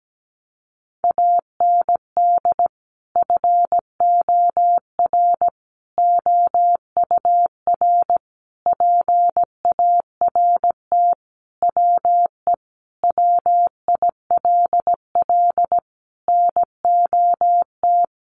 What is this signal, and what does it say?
Morse code, heard as a keyed tone, about 17 wpm, 700 Hz. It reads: AND FOR OUR PART WE WILL NOT
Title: Sentences from Top 1100 Words 17wpm